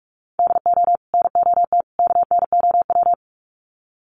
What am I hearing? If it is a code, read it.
Morse code, heard as a keyed tone, about 46 wpm, 700 Hz. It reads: DO NOT KNOW
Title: Sets of 3 Words 45wpm